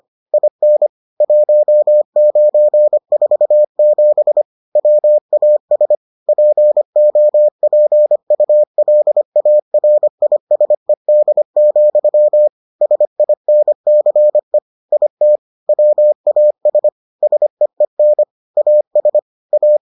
Morse code, heard as a keyed tone, about 25 wpm, 600 Hz. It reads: IN 1947 WAS POPULARISED, SINCE IT WAS SEEN AS A